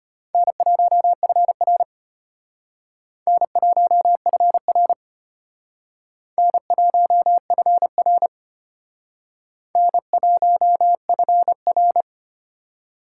Morse code, heard as a keyed tone, about 32 wpm, 700 Hz. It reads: N1FR N1FR N1FR N1FR